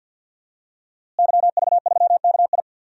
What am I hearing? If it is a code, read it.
Morse code, heard as a keyed tone, about 46 wpm, 700 Hz. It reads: YV3KI
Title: DXCC-100 Call Signs - 5 Characters - Repeated 2x - 50wpm